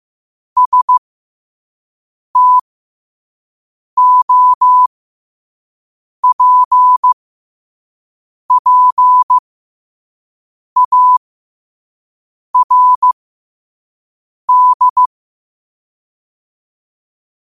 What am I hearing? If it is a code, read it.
Morse code, heard as a keyed tone, about 15 wpm, spaced as for 5 wpm, 1000 Hz. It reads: STOPPARD
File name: qotdfiles / 5wpm